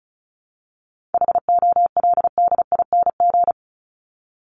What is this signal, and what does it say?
Morse code, heard as a keyed tone, about 34 wpm, 700 Hz. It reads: HOLDING
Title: ING Suffix - Top 100 Words with Base Words 35wpm